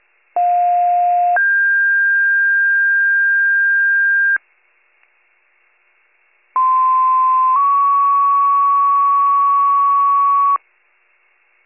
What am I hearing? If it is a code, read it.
Two-tone page, heard as a one-second tone, then a three-second tone, about 710 then 1600 Hz, about 1040 then 1120 Hz. It is heard from 0.4 to 10.6 s.